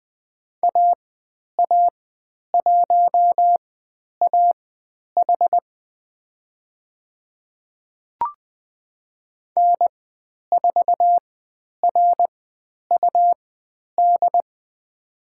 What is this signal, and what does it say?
Morse code, heard as a keyed tone, about 20 wpm, spaced as for 10 wpm, 700 Hz. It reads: AA1AH N4RUD
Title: 109 Lesson 18g - Letter m - Call Signs 20wpm